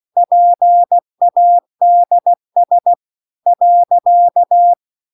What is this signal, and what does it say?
Morse code, heard as a keyed tone, about 16 wpm, 700 Hz. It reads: PADS .
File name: CW-The-Big-Twoheaded-River-2-16WPM